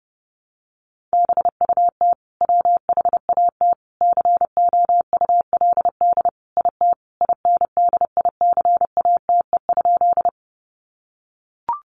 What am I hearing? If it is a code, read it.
Morse code, heard as a keyed tone, about 30 wpm, 700 Hz. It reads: BUT WHAT COULD IT INDICATE?
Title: Sentences from Top 1300 Words 30wpm